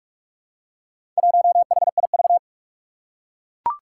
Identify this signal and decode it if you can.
Morse code, heard as a keyed tone, about 44 wpm, 700 Hz. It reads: 1HIV
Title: Four Challenging Letter-Number Combinations 45wpm